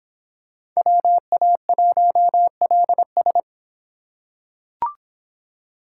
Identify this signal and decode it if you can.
Morse code, heard as a keyed tone, about 26 wpm, 700 Hz. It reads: WA1LS